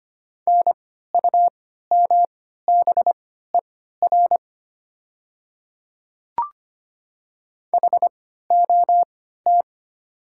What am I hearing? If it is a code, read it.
Morse code, heard as a keyed tone, about 25 wpm, spaced as for 14 wpm, 700 Hz. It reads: NUMBER HOT
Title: 3x Character Spacing - Top 100 Words - 25wpm